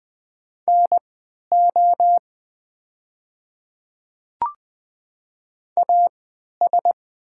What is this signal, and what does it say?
Morse code, heard as a keyed tone, about 20 wpm, spaced as for 11 wpm, 700 Hz. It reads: NO AS
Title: 3x Character Spacing - Top 100 Words - 20wpm